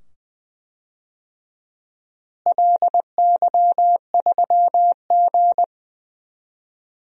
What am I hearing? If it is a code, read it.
Morse code, heard as a keyed tone, about 20 wpm, 700 Hz. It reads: LY3G
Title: DXCC-100 Call Signs - 4 Characters - Repeated 2x - 20wpm